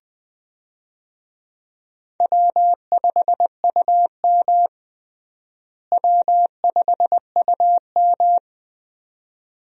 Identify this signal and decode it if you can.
Morse code, heard as a keyed tone, about 20 wpm, 700 Hz. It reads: W5UM W5UM
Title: USA Call Signs - 4 Characters - Repeated 2x - 20wpm